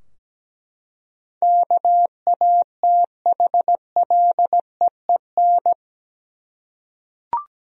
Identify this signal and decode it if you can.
Morse code, heard as a keyed tone, about 17 wpm, 700 Hz. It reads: KATHLEEN
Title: USA Common Names - 17wpm